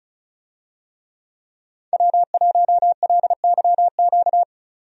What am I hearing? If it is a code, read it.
Morse code, heard as a keyed tone, about 36 wpm, 700 Hz. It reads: W1LYQ
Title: USA Call Signs - 5 Characters - 35wpm